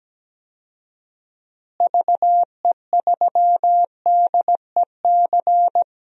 Morse code, heard as a keyed tone, about 17 wpm, 700 Hz. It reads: VE3DEC
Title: DXCC-100 Call Signs - 6 Characters - Repeated 3x - 17wpm